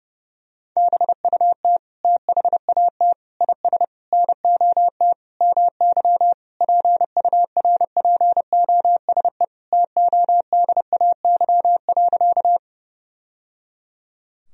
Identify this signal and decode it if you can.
Morse code, heard as a keyed tone, about 30 wpm, 700 Hz. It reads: BUT THAT IS NOT MY PURPOSE TODAY.
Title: Sentences from Top 1000 Words - Review Part 4 - 30wpm